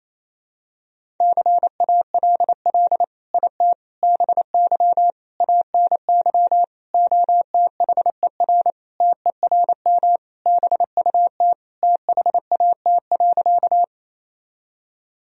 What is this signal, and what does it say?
Morse code, heard as a keyed tone, about 28 wpm, 700 Hz. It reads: CALL IT BY ANY OTHER TERM BUT THAT.
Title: Sentences from Top 1000 Words - Review Part 4 - 28wpm